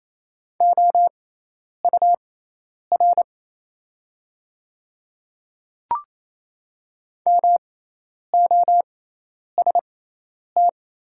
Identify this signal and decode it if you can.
Morse code, heard as a keyed tone, about 28 wpm, spaced as for 10 wpm, 690 Hz. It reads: OUR MOST
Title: 6x Character Spacing - Top 200 Words - 28wpm